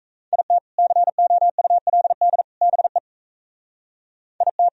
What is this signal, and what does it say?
Morse code, heard as a keyed tone, about 33 wpm, 700 Hz. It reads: IT COULD BE IT
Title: Speed-Racing - Sets of 3 Words 28wpm